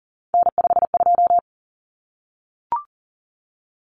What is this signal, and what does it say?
Morse code, heard as a keyed tone, about 40 wpm, 700 Hz. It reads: N52
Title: Common Cut Numbers - 3-Digit 40wpm